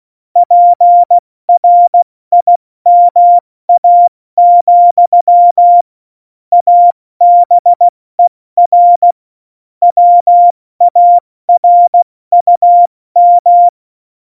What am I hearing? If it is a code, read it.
Morse code, heard as a keyed tone, about 16 wpm, 700 Hz. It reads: PRIMA, ABER WARUM
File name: Morsekurs / Witze / Witz 15. Januar 2019, 70 F80